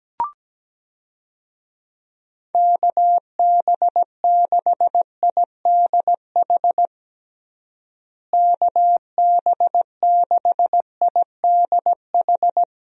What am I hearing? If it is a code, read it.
Morse code, heard as a keyed tone, about 17 wpm, 700 Hz. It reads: KB6IDH KB6IDH
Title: USA Call Signs - 6 Characters - Repeated 2x - 17wpm